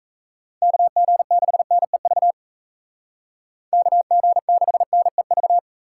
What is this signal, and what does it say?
Morse code, heard as a keyed tone, about 36 wpm, 700 Hz. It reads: KG6NEV KG6NEV